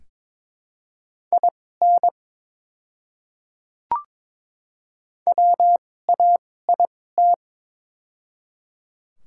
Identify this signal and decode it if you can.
Morse code, heard as a keyed tone, about 22 wpm, spaced as for 16 wpm, 700 Hz. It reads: IN WAIT
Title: 2x Character Spacing - Top 500 Words - 22wpm